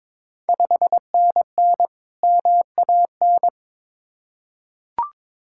Morse code, heard as a keyed tone, about 21 wpm, 700 Hz. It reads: 5NN MAN
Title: Ontario QSO Party Exchange 22wpm